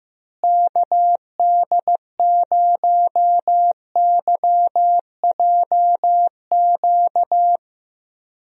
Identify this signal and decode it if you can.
Morse code, heard as a keyed tone, about 15 wpm, 700 Hz. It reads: KD0YJQ
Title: USA Call Signs - 6 Characters - Repeated 3x - 15wpm